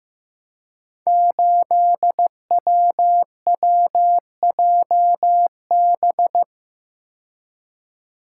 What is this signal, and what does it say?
Morse code, heard as a keyed tone, about 15 wpm, 700 Hz. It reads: 8WWJB
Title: Code Groups - 5 Letter:Numbers - 15wpm